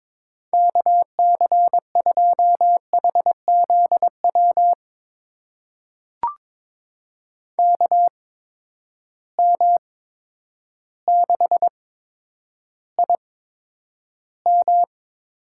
Morse code, heard as a keyed tone, about 22 wpm, spaced as for 10 wpm, 700 Hz. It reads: KC2HZW KM6IM
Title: USA Call Signs - 6 Characters - 8x Character Spacing - 22wpm